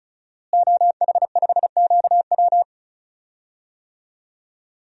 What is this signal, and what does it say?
Morse code, heard as a keyed tone, about 34 wpm, 700 Hz. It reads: OH5QW
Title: DXCC-100 Call Signs - 5 Characters - Repeated 2x - 35wpm